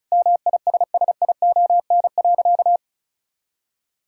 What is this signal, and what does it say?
Morse code, heard as a keyed tone, about 34 wpm, 700 Hz. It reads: MISSION.
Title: Sentences from Top 1600 Words 35wpm